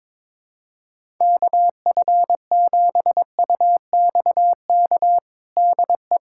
Morse code, heard as a keyed tone, about 22 wpm, 700 Hz. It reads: KF7UXK DE
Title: USA Callsign DE Callsign 22wpm